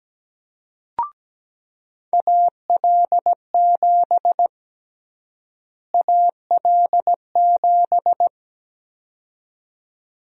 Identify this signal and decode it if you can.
Morse code, heard as a keyed tone, about 17 wpm, 700 Hz. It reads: AL7 AL7
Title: USA Call Sign Prefixes - Repeated 2x - 17wpm